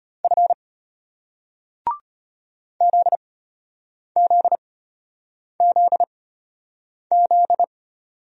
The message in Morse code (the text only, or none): F ZZZZ